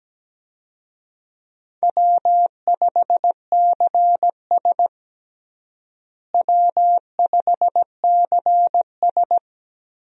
Morse code, heard as a keyed tone, about 17 wpm, 700 Hz. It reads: W5CS W5CS
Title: USA Call Signs - 4 Characters - Repeated 2x - 17wpm